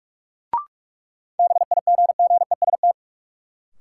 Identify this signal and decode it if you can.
Morse code, heard as a keyed tone, about 46 wpm, 700 Hz. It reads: BIGGEST
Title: Sets of 4 Words - New Word Drill - 45wpm